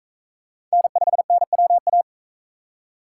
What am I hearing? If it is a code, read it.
Morse code, heard as a keyed tone, about 44 wpm, 700 Hz. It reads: N5NWA